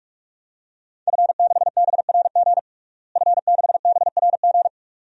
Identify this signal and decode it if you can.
Morse code, heard as a keyed tone, about 47 wpm, 700 Hz. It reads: F6BRG F6BRG
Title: DXCC-100 Call Signs - 5 Characters - Repeated 2x - 45wpm